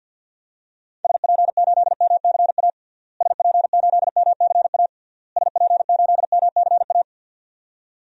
Morse code, heard as a keyed tone, about 48 wpm, 700 Hz. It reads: SP8MCA SP8MCA SP8MCA
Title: DXCC-100 Call Signs - 6 Characters - Repeated 3x - 50wpm